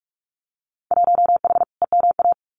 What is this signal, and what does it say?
Morse code, heard as a keyed tone, about 46 wpm, 700 Hz. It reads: 1H EMA
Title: Winter Field Day Exchanges 45wpm